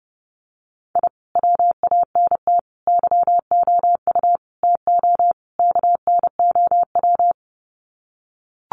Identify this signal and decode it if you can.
Morse code, heard as a keyed tone, about 30 wpm, 700 Hz. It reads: I WANT YOU TO KNOW
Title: Sets of 5 Words Encore 30wpm